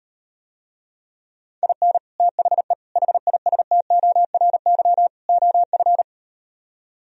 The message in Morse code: IN THE HISTORY OF

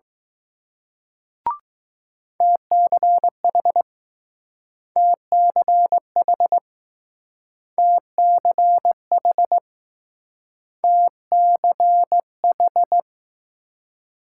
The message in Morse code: TCH TCH TCH TCH

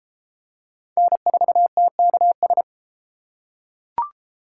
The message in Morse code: N4TKS

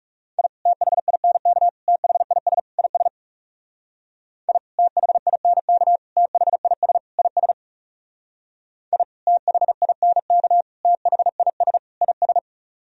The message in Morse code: I THINK THIS IS I THINK THIS IS I THINK THIS IS

